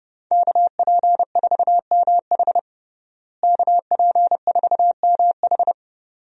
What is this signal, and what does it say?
Morse code, heard as a keyed tone, about 30 wpm, 700 Hz. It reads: KP4MH KP4MH
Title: DXCC-100 Call Signs - 5 Characters - Repeated 2x - 30wpm